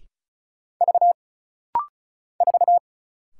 Morse code, heard as a keyed tone, about 34 wpm, 700 Hz. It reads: V 4
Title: Single Letter-Number - Rapid-Fire 35wpm